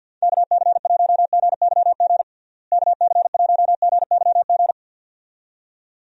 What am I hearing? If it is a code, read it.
Morse code, heard as a keyed tone, about 51 wpm, 700 Hz. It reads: KC1GYG KC1GYG
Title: USA Call Signs - 6 Characters - Repeated 2x - 50wpm